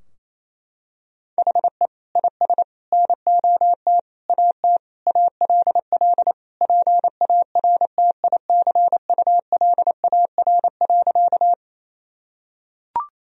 Morse code, heard as a keyed tone, about 28 wpm, 700 Hz. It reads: HE IS NOT AT ALL PARTICULAR.
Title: Sentences from Top 1000 Words - Review Part 4 - 28wpm